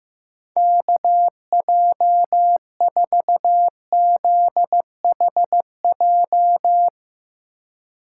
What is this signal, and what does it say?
Morse code, heard as a keyed tone, about 15 wpm, 700 Hz. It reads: KJ4ZHJ
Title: USA Call Signs - 6 Characters - 2x Character Spacing - 15wpm